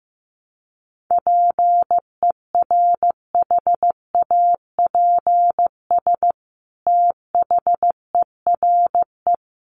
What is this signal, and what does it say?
Morse code, heard as a keyed tone, about 15 wpm, 700 Hz. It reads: PERHAPS THERE
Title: Sentences from Top 600 Words 15wpm